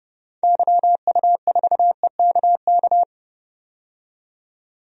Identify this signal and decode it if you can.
Morse code, heard as a keyed tone, about 32 wpm, 700 Hz. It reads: YU4EKK
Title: DXCC-100 Call Signs - 6 Characters - 30wpm